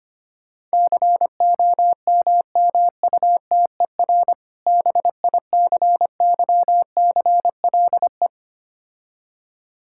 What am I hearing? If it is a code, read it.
Morse code, heard as a keyed tone, about 25 wpm, 700 Hz. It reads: COMMUTER BICYCLE
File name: practice / 25wpm / Difficult Words - Speed-Racing with Hints - 25wpm